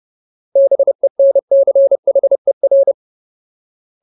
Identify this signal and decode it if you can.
Morse code, heard as a keyed tone, about 30 wpm, 550 Hz. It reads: BENCHER
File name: Ultimate Word List / 30WPM Ultimate